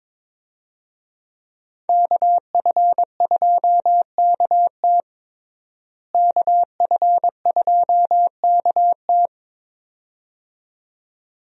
Morse code, heard as a keyed tone, about 22 wpm, 700 Hz. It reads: KF2KT KF2KT
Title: USA Call Signs - 5 Characters - Repeated 2x - 22wpm